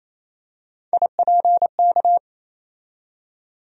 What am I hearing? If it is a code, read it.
Morse code, heard as a keyed tone, about 28 wpm, 700 Hz. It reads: IPK
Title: USA Call Sign Postfixes - 3 Letters 28wpm